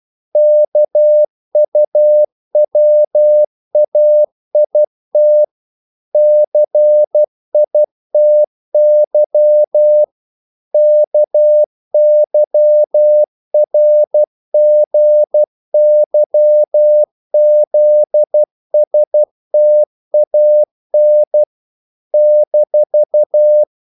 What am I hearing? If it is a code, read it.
Morse code, heard as a keyed tone, about 12 wpm, 600 Hz. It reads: KUWAIT CITY KYRGYZSTAN -
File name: CW / Practice / NationalCapitals12